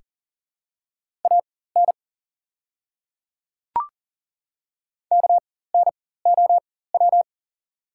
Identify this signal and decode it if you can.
Morse code, heard as a keyed tone, about 40 wpm, spaced as for 19 wpm, 690 Hz. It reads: AN KNOW
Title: 4x Character Spacing - Top 100 Words - 40wpm